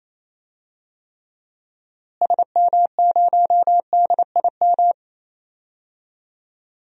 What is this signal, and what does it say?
Morse code, heard as a keyed tone, about 28 wpm, 700 Hz. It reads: SM0DIM
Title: DXCC-100 Call Signs - 6 Characters - Repeated 2x - 28wpm